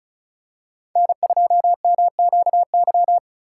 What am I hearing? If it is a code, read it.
Morse code, heard as a keyed tone, about 36 wpm, 700 Hz. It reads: N2MQY
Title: USA Call Signs - 5 Characters - 3x Character Spacing - 35wpm